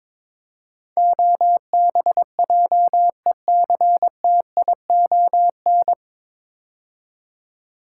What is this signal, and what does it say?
Morse code, heard as a keyed tone, about 22 wpm, 700 Hz. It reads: OBJECTION
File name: 9 Letter Words - First 3 Letters Given - 22wpm